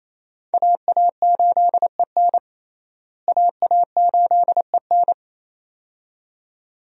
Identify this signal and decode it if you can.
Morse code, heard as a keyed tone, about 28 wpm, 700 Hz. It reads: AA8EN AA8EN
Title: USA Call Signs - 5 Characters - Repeated 2x - 28wpm